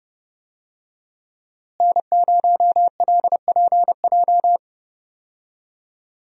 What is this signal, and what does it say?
Morse code, heard as a keyed tone, about 32 wpm, 700 Hz. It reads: N0LPJ
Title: USA Call Signs - 5 Characters - Repeated 2x - 30wpm